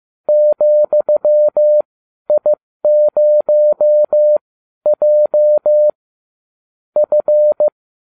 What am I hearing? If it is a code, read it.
Morse code, heard as a keyed tone, about 15 wpm, spaced as for 12 wpm, 600 Hz. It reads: ,I0J F